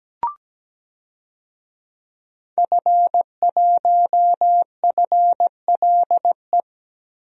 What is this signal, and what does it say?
Morse code, heard as a keyed tone, about 17 wpm, 700 Hz. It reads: F1FLE
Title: DXCC-100 Call Signs - 5 Characters - Repeated 2x - 17wpm